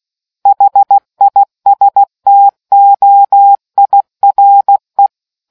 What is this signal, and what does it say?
Morse code, heard as a keyed tone, about 16 wpm, 800 Hz. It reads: HISTOIRE